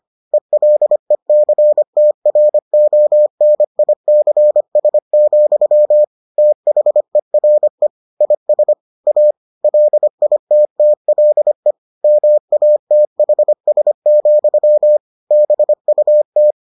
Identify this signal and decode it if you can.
Morse code, heard as a keyed tone, about 25 wpm, 600 Hz. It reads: ELECTRONICS, THERE IS A LITTLE MATHS, BUT